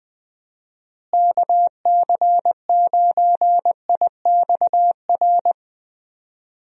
Morse code, heard as a keyed tone, about 20 wpm, 700 Hz. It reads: KC9IXR
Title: USA Call Signs - 6 Characters - Repeated 3x - 20wpm